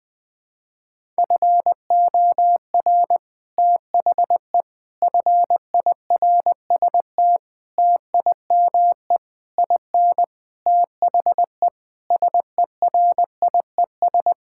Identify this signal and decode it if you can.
Morse code, heard as a keyed tone, about 21 wpm, 700 Hz. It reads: FOR THE FIRST TIME IN THE SERIES
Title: Sentences with 6-Word Phrases - 20wpm